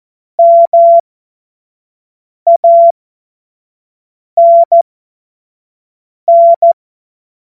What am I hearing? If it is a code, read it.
Morse code, heard as a keyed tone, about 14 wpm, spaced as for 5 wpm, 680 Hz. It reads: MANN
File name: Morsekurs / Witze / Witz 01. Februar 2022 25 F70